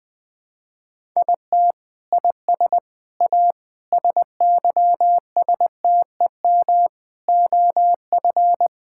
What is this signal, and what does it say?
Morse code, heard as a keyed tone, about 20 wpm, 700 Hz. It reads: IT IS A SYSTEM OF
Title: Sentences from Top 1000 Words - Review Part 6 - 20wpm